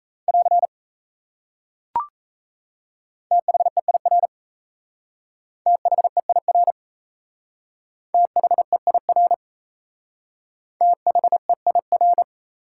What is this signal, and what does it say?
Morse code, heard as a keyed tone, about 34 wpm, 700 Hz. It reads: <AR> THEIR THEIR THEIR THEIR
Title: Speed-Racing Top 180 QSO Elements 28wpm